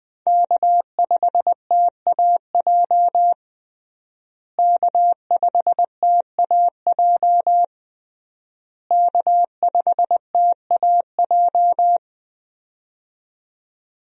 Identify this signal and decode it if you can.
Morse code, heard as a keyed tone, about 21 wpm, 700 Hz. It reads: K5TAJ K5TAJ K5TAJ
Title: USA Call Signs - 5 Characters - Repeated 3x - 20wpm